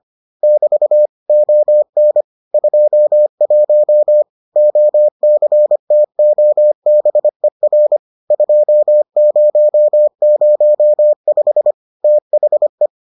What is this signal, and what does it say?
Morse code, heard as a keyed tone, about 25 wpm, 600 Hz. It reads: <BT> ON 21 OCTOBER 2005 THE